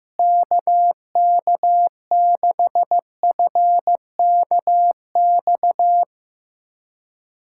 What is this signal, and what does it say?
Morse code, heard as a keyed tone, about 15 wpm, 700 Hz. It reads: KK6FKX
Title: USA Callsign DE Callsign - 6 Characters - 15wpm